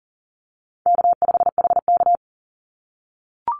Morse code, heard as a keyed tone, about 40 wpm, 700 Hz. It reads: K5HK
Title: Easy Call Signs - 40wpm